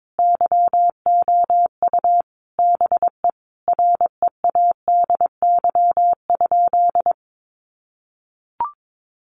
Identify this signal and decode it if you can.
Morse code, heard as a keyed tone, about 22 wpm, 700 Hz. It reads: YOU BE READY?
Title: Sentences from Top 1200 Words 22wpm